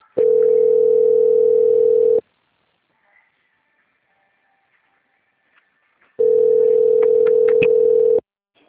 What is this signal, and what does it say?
Ringback tone (North America), 440 + 480 Hz.
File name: out- - - - - .7488